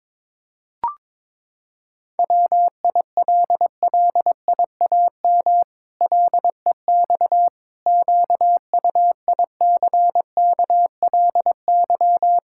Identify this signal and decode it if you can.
Morse code, heard as a keyed tone, about 22 wpm, 700 Hz. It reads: WILLIAM LEX QUICKLY
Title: Pangram Challenge v2 - 22wpm